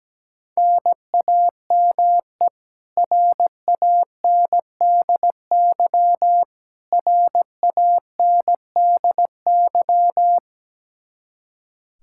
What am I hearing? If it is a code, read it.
Morse code, heard as a keyed tone, about 17 wpm, 700 Hz. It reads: NAME RANDY RANDY
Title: Practice QSO - Exchange 1 - Beginning - 17wpm